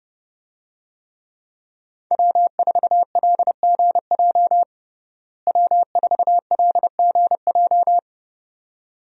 Morse code, heard as a keyed tone, about 30 wpm, 700 Hz. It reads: W4LGJ W4LGJ
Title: USA Call Signs - 5 Characters - Repeated 2x - 30wpm